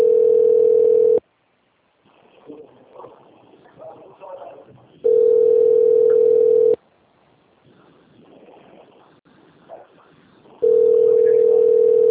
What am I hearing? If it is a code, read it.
Ringback tone (North America), 440 + 480 Hz, with speech.
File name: out- - - - - .27832